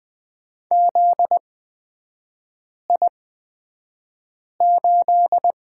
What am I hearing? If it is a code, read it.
Morse code, heard as a keyed tone, about 20 wpm, spaced as for 5 wpm, 700 Hz. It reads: ZI8
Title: 243 Lesson 39e - Letter z - Three Characters 20wpm